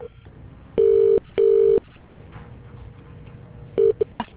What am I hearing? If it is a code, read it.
Ringback tone (UK), 400 + 450 Hz.